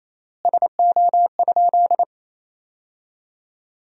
Morse code, heard as a keyed tone, about 28 wpm, 700 Hz. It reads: SO?
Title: 3x Word-Spacing - Sentences from Top 200 Words - 28wpm